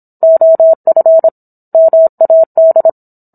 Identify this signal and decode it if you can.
Morse code, heard as a keyed tone, about 27 wpm, 650 Hz. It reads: OF MAD